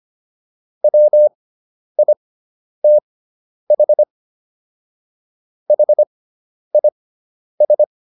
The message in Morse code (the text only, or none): WITH HIS